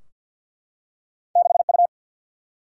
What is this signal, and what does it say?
Morse code, heard as a keyed tone, about 52 wpm, 700 Hz. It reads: 6U
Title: Single-Letter Single-Number 50wpm